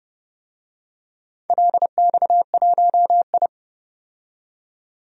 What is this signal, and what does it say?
Morse code, heard as a keyed tone, about 30 wpm, 700 Hz. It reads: LX1I
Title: DXCC-100 Call Signs - 4 Characters - 30wpm